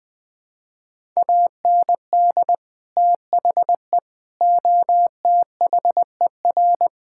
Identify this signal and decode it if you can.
Morse code, heard as a keyed tone, about 20 wpm, 700 Hz. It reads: AND THE OTHER
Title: Build A Phrase - 3-Word Phrases - 20wpm